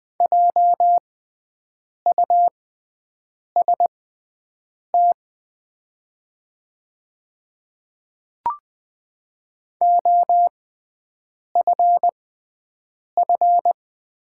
Morse code, heard as a keyed tone, about 20 wpm, spaced as for 7 wpm, 700 Hz. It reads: JUST OFF
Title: 6x Character Spacing - Top 500 Words - 20wpm